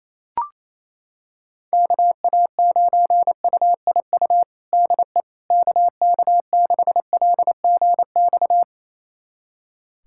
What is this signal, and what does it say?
Morse code, heard as a keyed tone, about 28 wpm, 700 Hz. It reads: KA9UIU DE KK6LGX
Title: USA Callsign DE Callsign - 6 Characters - 28wpm